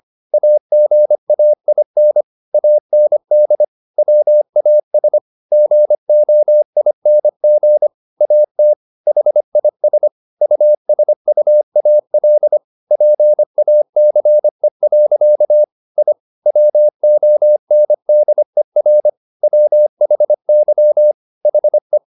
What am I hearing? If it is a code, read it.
Morse code, heard as a keyed tone, about 25 wpm, 600 Hz. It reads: AGAIN AND WAS GOING AT HIS USUAL PACE. I WONDER WHY HE
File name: the old man and the sea part 1 25wpm